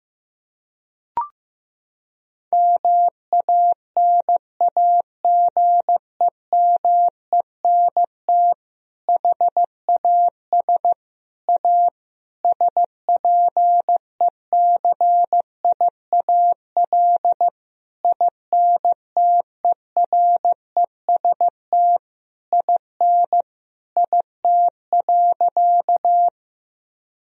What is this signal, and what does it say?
Morse code, heard as a keyed tone, about 15 wpm, 700 Hz. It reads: MANAGEMENT HAS A SPECIAL INTEREST IN IT.
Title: Sentences from Top 1000 Words - Review Part 4 - 15wpm